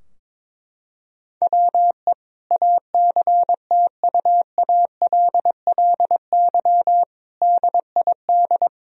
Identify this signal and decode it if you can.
Morse code, heard as a keyed tone, about 21 wpm, 700 Hz. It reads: WE ACTUALLY DID
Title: Sentences from Top 1000 Words - Review Part 4 - 22wpm